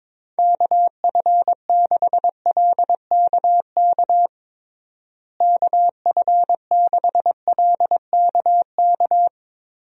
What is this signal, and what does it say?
Morse code, heard as a keyed tone, about 22 wpm, 700 Hz. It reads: KF6LKK KF6LKK
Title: USA Call Signs - 6 Characters - Repeated 2x - 22wpm